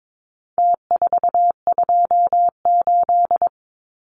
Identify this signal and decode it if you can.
Morse code, heard as a keyed tone, about 22 wpm, 700 Hz. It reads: T428
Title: Numbers - Contest Style - 4-Digit 22wpm